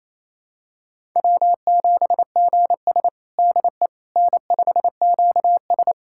Morse code, heard as a keyed tone, about 29 wpm, 690 Hz. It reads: W7GS DE N5QS
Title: USA Callsign DE Callsign - 4 Characters - 28wpm